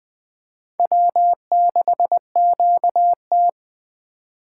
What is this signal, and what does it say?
Morse code, heard as a keyed tone, about 21 wpm, 700 Hz. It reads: W6QT